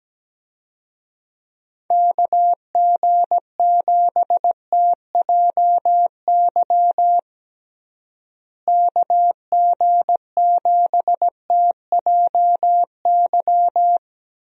Morse code, heard as a keyed tone, about 17 wpm, 700 Hz. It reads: KG7TJY KG7TJY
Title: USA Call Signs - 6 Characters - Repeated 3x - 17wpm